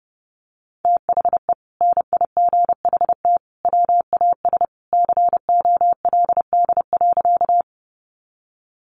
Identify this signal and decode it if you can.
Morse code, heard as a keyed tone, about 30 wpm, 700 Hz. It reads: THE NIGHT WAS COLD.
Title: Sentences from Top 400 Words 30wpm